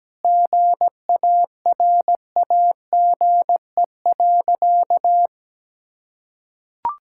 Morse code, heard as a keyed tone, about 17 wpm, 700 Hz. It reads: GARAGE.
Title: Sentences from Top 1600 Words 17wpm